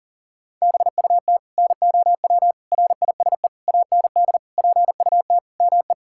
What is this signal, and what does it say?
Morse code, heard as a keyed tone, about 40 wpm, 700 Hz. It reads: BUT NOW RISE AND PUT ME
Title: Sentences from Top 1000 Words - Review Part 6 - 40wpm